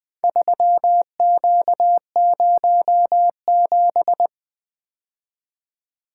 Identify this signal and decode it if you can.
Morse code, heard as a keyed tone, about 20 wpm, 700 Hz. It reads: 3Q07
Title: Four Letter-Number Combinations - 20wpm